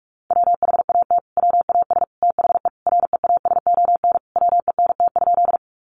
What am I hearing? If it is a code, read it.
Morse code, heard as a keyed tone, about 44 wpm, 700 Hz. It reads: WHAT WAS THE REASON WENT?
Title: Sentences from Top 700 Words 45wpm